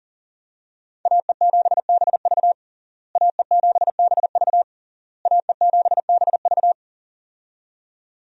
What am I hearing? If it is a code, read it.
Morse code, heard as a keyed tone, about 40 wpm, 700 Hz. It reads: AE7BV AE7BV AE7BV
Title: USA Call Signs - 5 Characters - Repeated 3x - 40wpm